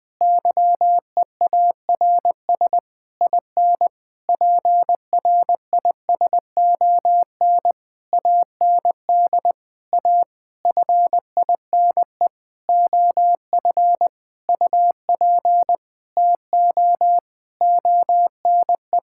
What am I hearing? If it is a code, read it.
Morse code, heard as a keyed tone, about 20 wpm, 700 Hz. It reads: YEARS IN PRISON AND A FINE OF UP TO ONE
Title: Sentences with 9-Word Phrases - Spoken Beginning - 20wpm